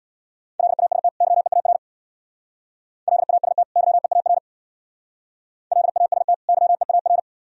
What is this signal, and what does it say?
Morse code, heard as a keyed tone, about 66 wpm, 700 Hz. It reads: LAST YEAR LAST YEAR LAST YEAR